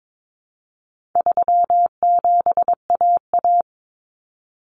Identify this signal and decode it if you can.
Morse code, heard as a keyed tone, about 22 wpm, 700 Hz. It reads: 37AA
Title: Common Cut Numbers - 4-Digit 22wpm